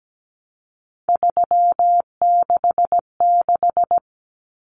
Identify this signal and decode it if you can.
Morse code, heard as a keyed tone, about 17 wpm, 700 Hz. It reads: 366
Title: Numbers - 3-Digit 17wpm